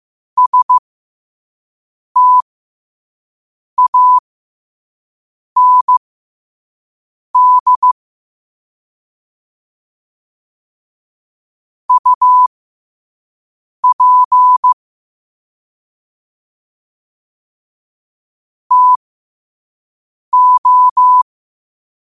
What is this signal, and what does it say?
Morse code, heard as a keyed tone, about 15 wpm, spaced as for 5 wpm, 1000 Hz. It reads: STAND UP TO
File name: qotdfiles / 5wpm